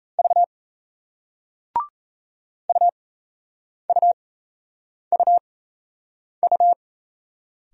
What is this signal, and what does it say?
Morse code, heard as a keyed tone, about 38 wpm, spaced as for 8 wpm, 700 Hz. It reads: V UUUU